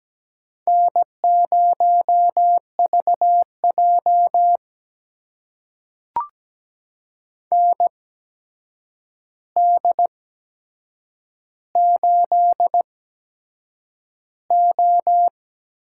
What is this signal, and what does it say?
Morse code, heard as a keyed tone, about 17 wpm, spaced as for 7 wpm, 700 Hz. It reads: N0VJ ND8O
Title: USA Call Signs - 4 Characters - 8x Character Spacing - 17wpm